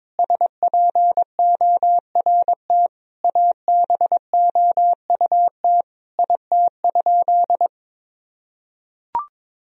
Morse code, heard as a keyed tone, about 22 wpm, 700 Hz. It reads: SPORT ABOUT IT?
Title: Sentences from Top 1000 Words - Review Part 4 - 22wpm